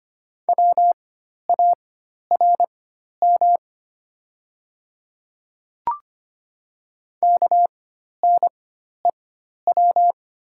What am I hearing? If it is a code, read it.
Morse code, heard as a keyed tone, about 25 wpm, spaced as for 12 wpm, 700 Hz. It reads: WARM KNEW
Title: 4x Character Spacing - Top 500 Words - 25wpm